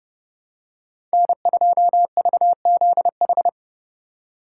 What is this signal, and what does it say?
Morse code, heard as a keyed tone, about 30 wpm, 700 Hz. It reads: N2VZH
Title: USA Call Signs - 5 Characters - 2x Character Spacing - 30wpm